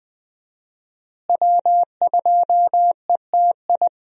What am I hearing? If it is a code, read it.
Morse code, heard as a keyed tone, about 21 wpm, 700 Hz. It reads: W2ETI
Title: USA Call Signs - 5 Characters - Repeated 2x - 20wpm